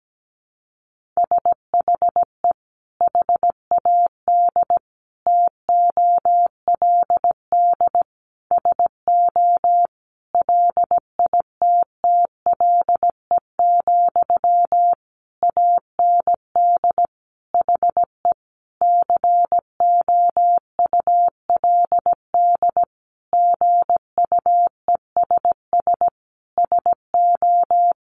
Morse code, heard as a keyed tone, about 17 wpm, 700 Hz. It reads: SHE HAD TOLD SO LITTLE, AND HE COULD GUESS SO
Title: Sentences from Top 1500 Words 17wpm